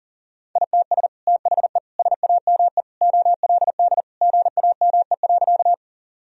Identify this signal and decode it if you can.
Morse code, heard as a keyed tone, about 40 wpm, 700 Hz. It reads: ITS THE SAME OLD GAME.
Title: Sentences from Top 1000 Words - Review Part 3 - 40wpm